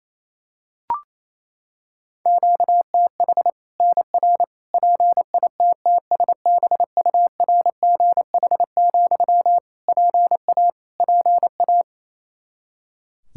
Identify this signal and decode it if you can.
Morse code, heard as a keyed tone, about 28 wpm, 700 Hz. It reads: QTH NR PITTSBURGH, PA PA
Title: Practice QSO - Exchange 1 - Beginning - 28wpm